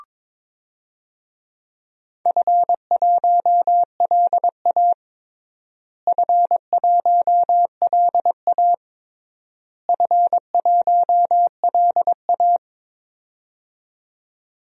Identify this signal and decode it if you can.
Morse code, heard as a keyed tone, about 21 wpm, 700 Hz. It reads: F1LA F1LA F1LA
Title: DXCC-100 Call Signs - 4 Characters - Repeated 3x - 22wpm